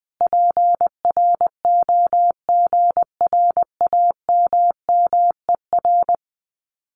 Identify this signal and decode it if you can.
Morse code, heard as a keyed tone, about 21 wpm, 700 Hz. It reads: PROGRAMMER
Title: Common QSO Phrases Encore 20wpm